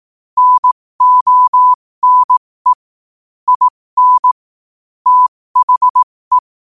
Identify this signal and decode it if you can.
Morse code, heard as a keyed tone, about 18 wpm, spaced as for 15 wpm, 1000 Hz. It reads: NONE IN THE
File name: qotdfiles / 15wpm